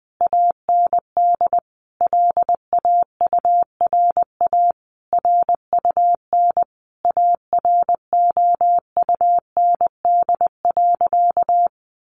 Morse code, heard as a keyed tone, about 20 wpm, 700 Hz. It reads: AND LAURA RUN AROUND.
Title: Additional Easy Phrases 20wpm